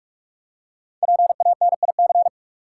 Morse code, heard as a keyed tone, about 45 wpm, 700 Hz. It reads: PANIC
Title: Five Letter Words - 45wpm